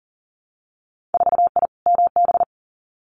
Morse code, heard as a keyed tone, about 40 wpm, 700 Hz. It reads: 4I MB
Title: Winter Field Day Exchanges Encore 40wpm